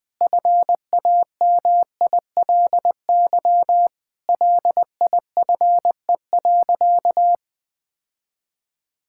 Morse code, heard as a keyed tone, about 20 wpm, 700 Hz. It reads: FAMILY LIFE.
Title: Sentences from Top 1000 Words - Review Part 5 - 20wpm